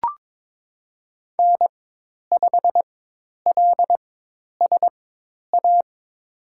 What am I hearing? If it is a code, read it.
Morse code, heard as a keyed tone, about 22 wpm, spaced as for 10 wpm, 700 Hz. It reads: N5LSA
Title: USA Call Signs - 5 Characters - 4x Character Spacing - 22wpm